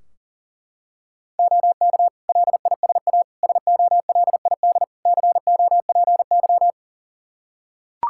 Morse code, heard as a keyed tone, about 37 wpm, 700 Hz. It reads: OK LISA SOLID COPY
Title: Practice QSO - Exchange 3 - More Info - 40wpm